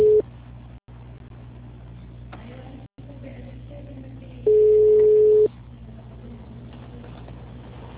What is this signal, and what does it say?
Ringback tone (Europe, 425 Hz).
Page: Piracicaba - SP